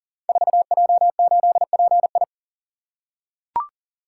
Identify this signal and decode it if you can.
Morse code, heard as a keyed tone, about 42 wpm, 690 Hz. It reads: 4J8PI
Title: Code Groups - 5 Letter:Numbers - 40wpm